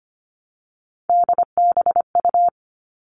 Morse code, heard as a keyed tone, about 25 wpm, 700 Hz. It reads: DBU
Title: All Cut Numbers - 3-Digit 25wpm